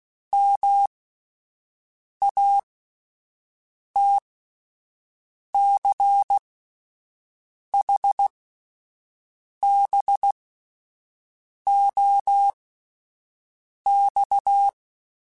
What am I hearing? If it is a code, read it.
Morse code, heard as a keyed tone, about 16 wpm, spaced as for 5 wpm, 800 Hz. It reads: MATCHBOX
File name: QSOs / QSOs-mp3s / WD0FBT 1@10